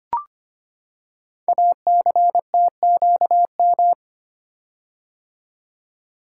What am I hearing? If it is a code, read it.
Morse code, heard as a keyed tone, about 25 wpm, 700 Hz. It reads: ACTQM
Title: Code Groups - 5 Letters - 25wpm